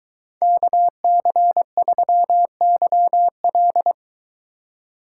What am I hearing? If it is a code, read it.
Morse code, heard as a keyed tone, about 22 wpm, 700 Hz. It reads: KC3YL